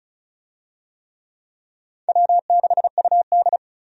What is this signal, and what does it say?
Morse code, heard as a keyed tone, about 34 wpm, 700 Hz. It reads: W6UD
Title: USA Call Signs - 4 Characters - 35wpm